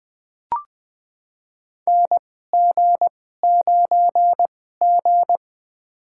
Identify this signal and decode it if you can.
Morse code, heard as a keyed tone, about 20 wpm, spaced as for 14 wpm, 700 Hz. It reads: NG9G
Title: USA Call Signs - 4 Characters - 2x Character Spacing - 20wpm